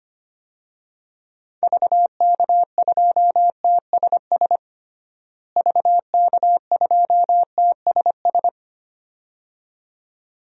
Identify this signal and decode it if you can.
Morse code, heard as a keyed tone, about 25 wpm, 700 Hz. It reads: VK2TSS VK2TSS
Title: DXCC-100 Call Signs - 6 Characters - Repeated 2x - 25wpm